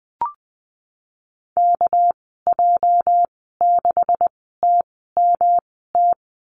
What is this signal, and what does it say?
Morse code, heard as a keyed tone, about 20 wpm, spaced as for 15 wpm, 700 Hz. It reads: KJ6TMT
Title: 219 Lesson 35g - Number 0 - Call Signs 20wpm